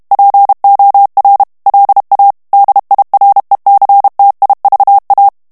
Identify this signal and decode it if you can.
Morse code, heard as a keyed tone, about 32 wpm, 800 Hz. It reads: POR LA DIRECTIVA
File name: ESPAGNOL TEXTE EN CLAIR 2 VITESSE 30 MOTS MINUTES